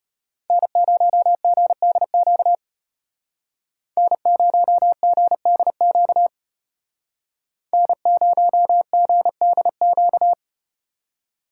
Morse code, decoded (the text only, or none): N0GDQ N0GDQ N0GDQ